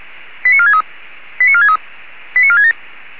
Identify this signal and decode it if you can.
Five-tone selective call (ZVEI), heard as quick runs of five tones, tones of about 70 ms. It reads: 89463 89463 89467